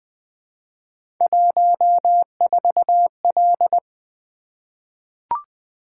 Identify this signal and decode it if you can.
Morse code, heard as a keyed tone, about 20 wpm, 700 Hz. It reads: 14L
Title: Three Letter-Number Combinations - 20wpm